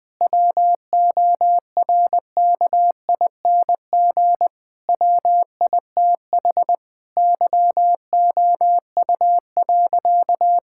Morse code, heard as a keyed tone, about 21 wpm, 700 Hz. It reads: WORKING WITH YOU.
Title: Sentences with 6-Word Phrases - 20wpm